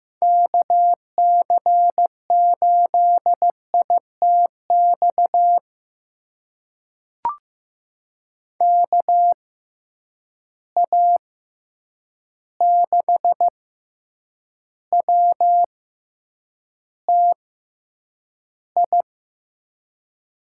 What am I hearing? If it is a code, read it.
Morse code, heard as a keyed tone, about 15 wpm, spaced as for 8 wpm, 700 Hz. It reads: KC8ITX KA6WTI
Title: USA Call Signs - 6x Character Spacing - 15wpm